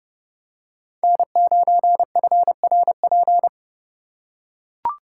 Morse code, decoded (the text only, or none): N9FRP